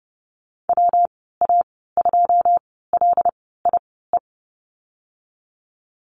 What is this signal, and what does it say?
Morse code, heard as a keyed tone, about 30 wpm, spaced as for 17 wpm, 700 Hz. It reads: WA2LIE
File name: course / 30:10wpm / 116 Lesson 19g - Letter w - Call Signs 30wpm